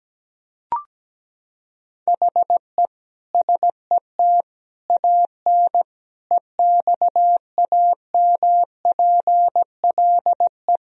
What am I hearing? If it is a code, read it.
Morse code, heard as a keyed tone, about 17 wpm, 700 Hz. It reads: HE SET AN EXAMPLE
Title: Sentences from Top 1000 Words - Review Part 1 - 17wpm